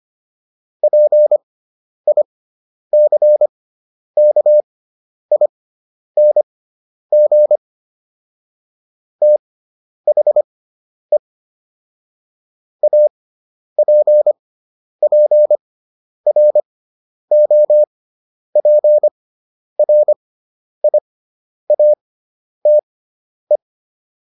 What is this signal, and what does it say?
Morse code, heard as a keyed tone, about 25 wpm, spaced as for 10 wpm, 600 Hz. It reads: PICKING THE APPROPRIATE